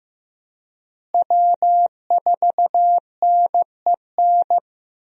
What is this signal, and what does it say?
Morse code, heard as a keyed tone, about 15 wpm, 700 Hz. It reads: W4NEN
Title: USA Call Signs - 5 Characters - Repeated 2x - 15wpm